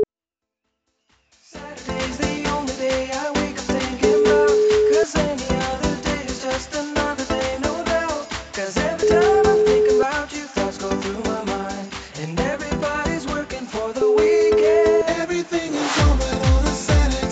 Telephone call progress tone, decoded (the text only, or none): ringback tone